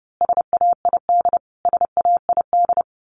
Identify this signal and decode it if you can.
Morse code, heard as a keyed tone, about 30 wpm, 700 Hz. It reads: SAID SAID
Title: Top 500 Words Repeated 30wpm